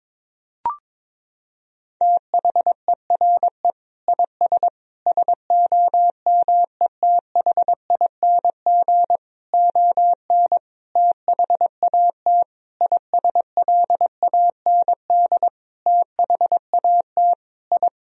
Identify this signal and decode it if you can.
Morse code, heard as a keyed tone, about 22 wpm, 690 Hz. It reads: THERE IS SOMETHING ON THAT ISLAND THAT I
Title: Sentences from Top 1000 Words - Review Part 3 - 22wpm